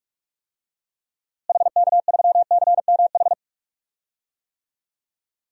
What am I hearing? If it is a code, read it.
Morse code, heard as a keyed tone, about 41 wpm, 700 Hz. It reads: HK3CMH
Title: DXCC-100 Call Signs - 6 Characters - Repeated 3x - 45wpm